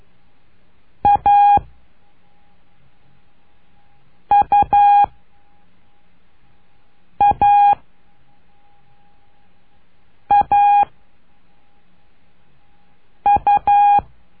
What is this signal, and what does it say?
Morse code, heard as a keyed tone, about 12 wpm, 820 Hz. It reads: A U A A U
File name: cursos / CURSO DE TELEGRAFIA / Telegrafia - Trilha 01